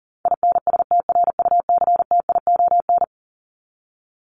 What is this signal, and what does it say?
Morse code, heard as a keyed tone, about 40 wpm, 700 Hz. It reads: INSTRUCTION
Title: Top 1751-2000 Words 40wpm